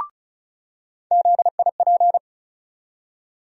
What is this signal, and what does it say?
Morse code, heard as a keyed tone, about 36 wpm, 700 Hz. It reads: ZIP
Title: USA Call Sign Postfixes - 3 Letters 35wpm